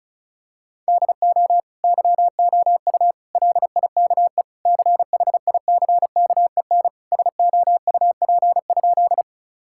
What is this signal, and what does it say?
Morse code, heard as a keyed tone, about 36 wpm, 700 Hz. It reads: DO YOU LIKE CHICKEN SOUP?
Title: Sentences from Top 1000 Words - Review Part 6 - 35wpm